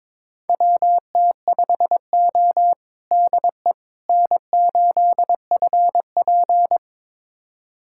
Morse code, heard as a keyed tone, about 21 wpm, 700 Hz. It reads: WT5O DE N8FP
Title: USA Callsign DE Callsign - 4 Characters - 22wpm